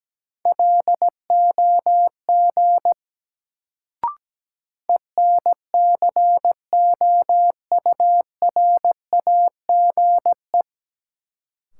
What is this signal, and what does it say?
Morse code, heard as a keyed tone, about 17 wpm, 700 Hz. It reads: LOG ENCOURAGE
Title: Sentences from Top 1400 Words - New Word Drill - 17wpm